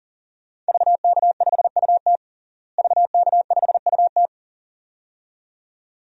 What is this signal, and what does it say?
Morse code, heard as a keyed tone, about 40 wpm, 700 Hz. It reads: VK5UT VK5UT
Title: DXCC-100 Call Signs - 5 Characters - Repeated 2x - 40wpm